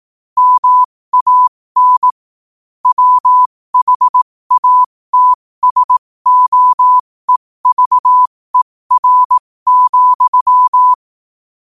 Morse code, heard as a keyed tone, about 18 wpm, spaced as for 15 wpm, 1000 Hz. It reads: MAN WHATSOEVER,
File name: qotdfiles / 15wpm